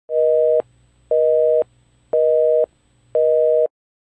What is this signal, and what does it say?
Busy tone (North America), 480 + 620 Hz, with noise.
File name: telephone busy signal